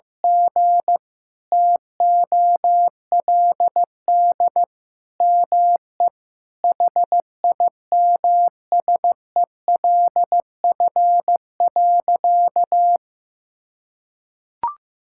Morse code, heard as a keyed tone, about 16 wpm, 700 Hz. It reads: G TOLD ME HIMSELF.
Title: Sentences from Top 1600 Words 15wpm